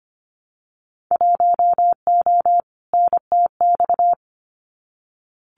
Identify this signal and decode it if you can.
Morse code, heard as a keyed tone, about 25 wpm, 700 Hz. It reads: 1O NTX
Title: Winter Field Day Exchanges 25wpm